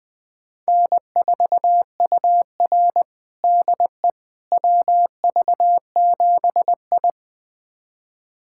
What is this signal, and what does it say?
Morse code, heard as a keyed tone, about 20 wpm, 700 Hz. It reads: N4UR DE WV7I
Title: USA Callsign DE Callsign - 4 Characters - 20wpm